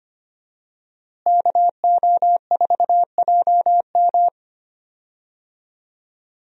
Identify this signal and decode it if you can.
Morse code, heard as a keyed tone, about 25 wpm, 700 Hz. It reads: KO4JM
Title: USA Call Signs - 5 Characters - Repeated 2x - 25wpm